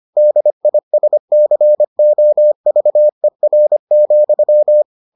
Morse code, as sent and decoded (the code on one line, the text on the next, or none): -.. .. ... -.-. --- ...- . .-. --..--
DISCOVER,